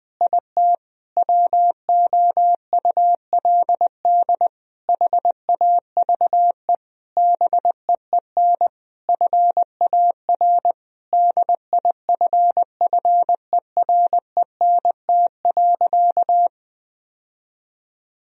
Morse code, heard as a keyed tone, about 21 wpm, 700 Hz. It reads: IT WOULD HAVE BEEN FAR DIFFERENT.
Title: Sentences from Top 1000 Words - Review Part 3 - 20wpm